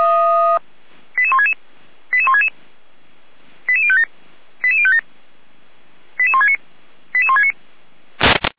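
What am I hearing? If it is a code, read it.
Five-tone selective call (ZVEI), heard as quick runs of five tones, tones of about 70 ms. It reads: 88166 88166 88057 88057 88169 88169
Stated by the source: Five-tone ZVEI selective call received on an SDR.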